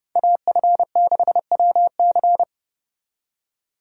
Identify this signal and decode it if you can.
Morse code, heard as a keyed tone, about 39 wpm, 700 Hz. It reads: AF6WC